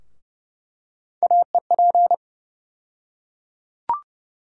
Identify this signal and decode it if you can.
Morse code, heard as a keyed tone, about 31 wpm, 700 Hz. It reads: AEP
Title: USA Call Sign Postfixes - 3 Letters 30wpm